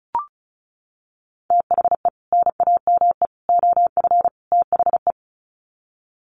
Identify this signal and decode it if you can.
Morse code, heard as a keyed tone, about 34 wpm, 700 Hz. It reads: THE NAME OF THE
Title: Sets of 4 Words 35wpm